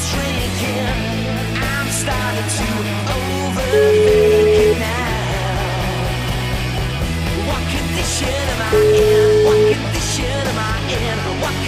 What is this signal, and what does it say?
Ringback tone (Europe, 425 Hz), with noise.